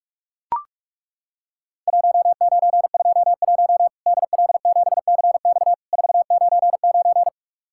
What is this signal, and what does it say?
Morse code, heard as a keyed tone, about 41 wpm, 700 Hz. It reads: 1921 DL7CX 499
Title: Worked All Europe DX Contest - QTC Exchange - 45wpm